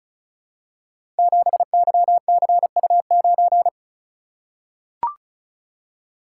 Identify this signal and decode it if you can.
Morse code, heard as a keyed tone, about 36 wpm, 700 Hz. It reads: 7YCU9
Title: Code Groups - 5 Letter:Numbers - 35wpm